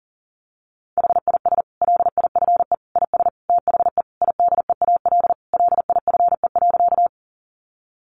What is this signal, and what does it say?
Morse code, heard as a keyed tone, about 40 wpm, 700 Hz. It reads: HIS LIFE IS THE IDEAL LIFE.
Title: Sentences from Top 1400 Words 40wpm